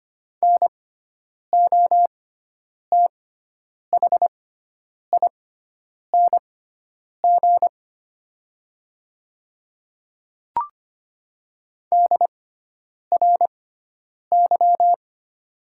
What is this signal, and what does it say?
Morse code, heard as a keyed tone, about 25 wpm, spaced as for 9 wpm, 700 Hz. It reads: NOTHING DRY
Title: 6x Character Spacing - Top 500 Words - 25wpm